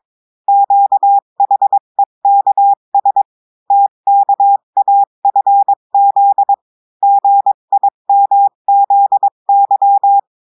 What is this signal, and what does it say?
Morse code, heard as a keyed tone, about 22 wpm, 800 Hz. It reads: QHEKS TKAFZ GIMZY
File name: cours-du-mercredi / audios / cours-853-du-2-mai-2024-etxte chiffre et texte clair-a-20-wpm